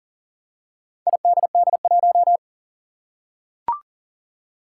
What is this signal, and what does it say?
Morse code, heard as a keyed tone, about 40 wpm, 700 Hz. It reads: IDD1
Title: Four Challenging Letter-Number Combinations 40wpm